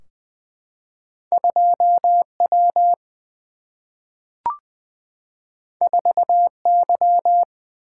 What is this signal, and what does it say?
Morse code, heard as a keyed tone, about 20 wpm, 700 Hz. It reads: 2 W 4 Y
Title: Single-Letter Single-Number 20wpm